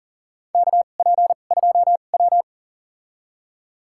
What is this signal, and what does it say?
Morse code, heard as a keyed tone, about 37 wpm, spaced as for 28 wpm, 700 Hz. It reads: KP2W
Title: USA Call Signs - 2x Character Spacing - 40wpm